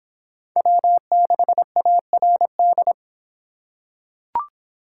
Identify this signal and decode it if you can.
Morse code, heard as a keyed tone, about 23 wpm, 700 Hz. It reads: W6ARD